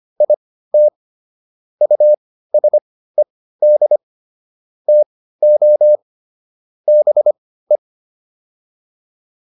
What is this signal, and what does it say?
Morse code, heard as a keyed tone, about 25 wpm, spaced as for 15 wpm, 600 Hz. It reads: IT USED TO BE